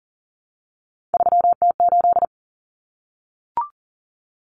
Morse code, heard as a keyed tone, about 40 wpm, 700 Hz. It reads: 3T8
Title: Common Cut Numbers - 3-Digit 40wpm